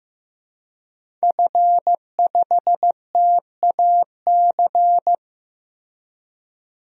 Morse code, heard as a keyed tone, about 15 wpm, 700 Hz. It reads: F5TAC
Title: DXCC-100 Call Signs - 5 Characters - Repeated 2x - 15wpm